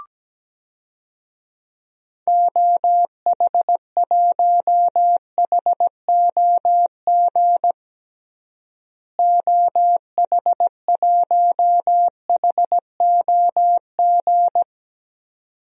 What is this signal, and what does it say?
Morse code, heard as a keyed tone, about 17 wpm, 700 Hz. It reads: OH1HOG OH1HOG
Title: DXCC-100 Call Signs - 6 Characters - Repeated 2x - 17wpm